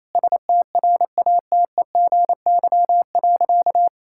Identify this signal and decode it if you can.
Morse code, heard as a keyed tone, about 27 wpm, 700 Hz. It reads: STRATEGY.
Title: Sentences from Top 1000 Words - Review Part 5 - 28wpm